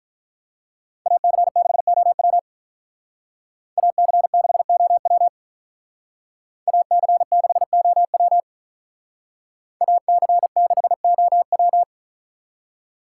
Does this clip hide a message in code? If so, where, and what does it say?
Yes: Morse code from 1.1 to 11.8 s, heard as a keyed tone, about 45 wpm, 700 Hz. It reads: AC6OW AC6OW AC6OW AC6OW